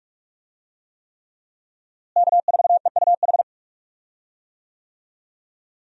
Morse code, heard as a keyed tone, about 46 wpm, 700 Hz. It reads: K4EUH
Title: USA Call Signs - 5 Characters - Repeated 3x - 45wpm